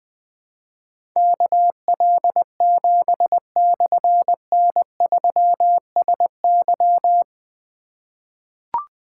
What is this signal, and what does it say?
Morse code, heard as a keyed tone, about 20 wpm, 700 Hz. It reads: KL7/N3SY
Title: Difficult Call Signs 20wpm